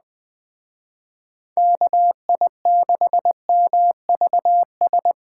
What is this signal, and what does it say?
Morse code, heard as a keyed tone, about 20 wpm, 700 Hz. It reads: KI6MVS